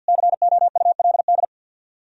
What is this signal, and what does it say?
Morse code, heard as a keyed tone, about 64 wpm, 700 Hz. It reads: COULD